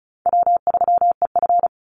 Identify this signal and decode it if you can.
Morse code, heard as a keyed tone, about 34 wpm, 700 Hz. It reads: W3EF
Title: Easy Call Signs Repeated 3x - 35wpm